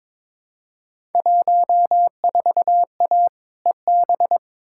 Morse code, heard as a keyed tone, about 22 wpm, 700 Hz. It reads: 14A EB
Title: Field Day Exchanges Encore - 22wpm